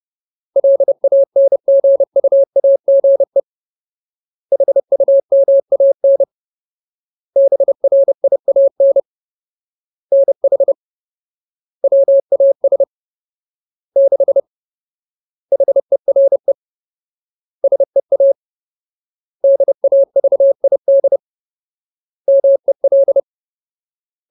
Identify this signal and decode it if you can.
Morse code, heard as a keyed tone, about 30 wpm, 550 Hz. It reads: LANGUAGE HUMAN BRIAN NH WAS 6 HERE SEA DAVID MEL
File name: Ultimate Word List / 30WPM Ultimate